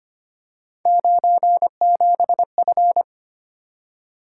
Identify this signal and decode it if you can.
Morse code, heard as a keyed tone, about 25 wpm, 700 Hz. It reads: 97F
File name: Three Letter-Number Combinations - 25wpm